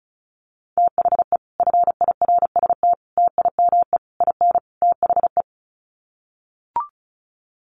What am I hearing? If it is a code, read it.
Morse code, heard as a keyed tone, about 34 wpm, 700 Hz. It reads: THE FIRST TIME IN THE
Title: Sets of 5 Words Encore 35wpm